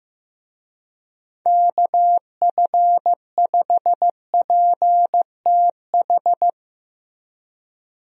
Morse code, heard as a keyed tone, about 15 wpm, 700 Hz. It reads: KF5PTH
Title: USA Call Signs - 6 Characters - Repeated 2x - 15wpm